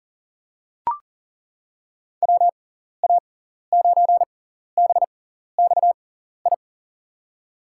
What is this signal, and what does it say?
Morse code, heard as a keyed tone, about 40 wpm, spaced as for 14 wpm, 700 Hz. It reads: WA9BXI
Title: USA Call Signs - 6 Characters - 6x Character Spacing - 40wpm